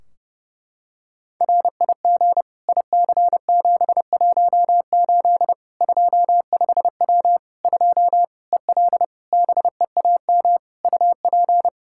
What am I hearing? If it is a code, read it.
Morse code, heard as a keyed tone, about 30 wpm, 700 Hz. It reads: RIG IC718 25W 2 EL BEAM UP
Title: Practice QSO - Exchange 2 - Info - 30wpm